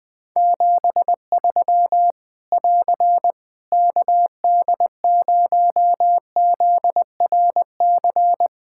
Morse code, heard as a keyed tone, about 21 wpm, 700 Hz. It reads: 73 <AR> KD0ZRC
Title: Practice QSO - Exchange 4 - Ending - 20wpm